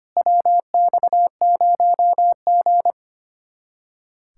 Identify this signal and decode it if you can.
Morse code, heard as a keyed tone, about 31 wpm, 700 Hz. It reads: WX0G
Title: Speed-Racing Common QSO Phrases - 25wpm